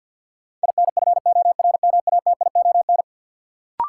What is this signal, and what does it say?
Morse code, heard as a keyed tone, about 47 wpm, 700 Hz. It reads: INFORMATION
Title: Sentences from Top 600 Words - New Word Drill - 50wpm